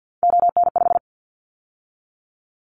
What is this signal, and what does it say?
Morse code, heard as a keyed tone, about 50 wpm, 700 Hz. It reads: ON5
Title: Callsign Prefixes for Top 100 DXCC - 50wpm